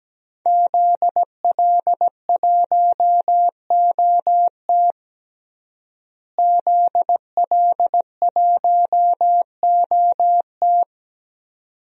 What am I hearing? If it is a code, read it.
Morse code, heard as a keyed tone, about 17 wpm, 700 Hz. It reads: ZL1OT ZL1OT
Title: DXCC-100 Call Signs - 5 Characters - Repeated 2x - 17wpm